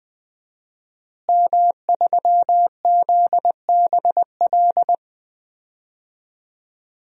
Morse code, heard as a keyed tone, about 21 wpm, 700 Hz. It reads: M3ZBL
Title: DXCC-100 Call Signs - 5 Characters - Repeated 2x - 20wpm